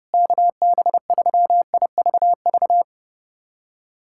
Morse code, heard as a keyed tone, about 30 wpm, 700 Hz. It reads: KB3IVV
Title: USA Callsign DE Callsign - 6 Characters - 30wpm